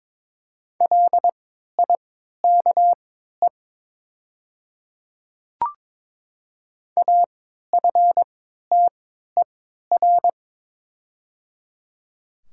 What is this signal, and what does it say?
Morse code, heard as a keyed tone, about 23 wpm, spaced as for 13 wpm, 700 Hz. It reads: LIKE AFTER
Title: 3x Character Spacing - Top 300 Words - 22wpm